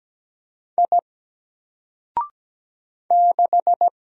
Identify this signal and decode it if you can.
Morse code, heard as a keyed tone, about 17 wpm, 700 Hz. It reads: I 6
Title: Single Character (Letter:Number:Punctuation:Prosign) 17wpm